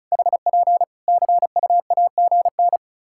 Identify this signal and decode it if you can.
Morse code, heard as a keyed tone, about 34 wpm, 700 Hz. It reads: HP CUAGN
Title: Practice QSO - Exchange 4 - Ending - 35wpm